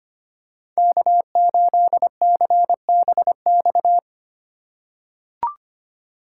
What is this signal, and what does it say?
Morse code, heard as a keyed tone, about 25 wpm, 700 Hz. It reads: K8CBX
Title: USA Call Signs - 4x Character Spacing - 25wpm